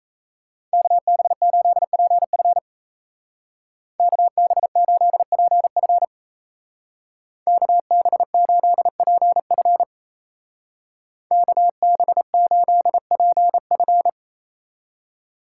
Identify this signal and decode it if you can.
Morse code, heard as a keyed tone, about 35 wpm, 700 Hz. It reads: KB8PF KB8PF KB8PF KB8PF